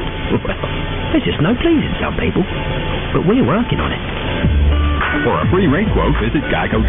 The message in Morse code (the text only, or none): none